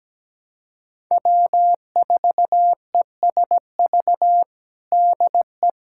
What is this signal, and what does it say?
Morse code, heard as a keyed tone, about 17 wpm, 700 Hz. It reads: W4ESV DE
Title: USA Callsign DE Callsign - 5 Characters - 17wpm